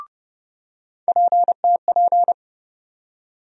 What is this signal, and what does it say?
Morse code, heard as a keyed tone, about 29 wpm, 700 Hz. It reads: PTP
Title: USA Call Sign Postfixes - 3 Letters 30wpm